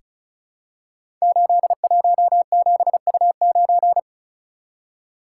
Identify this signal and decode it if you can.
Morse code, heard as a keyed tone, about 36 wpm, 690 Hz. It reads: 817U9
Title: Code Groups - 5 Letter:Numbers - 35wpm